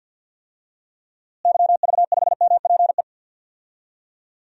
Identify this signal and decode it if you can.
Morse code, heard as a keyed tone, about 46 wpm, 700 Hz. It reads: YV5MPE
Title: DXCC-100 Call Signs - 6 Characters - Repeated 3x - 50wpm